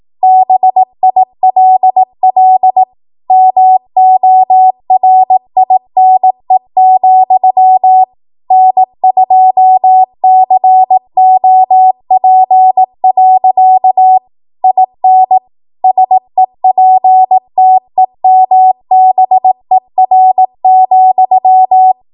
Morse code, heard as a keyed tone, about 18 wpm, 750 Hz. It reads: BILL MORINE, N2COP. IN SEPTEMBER,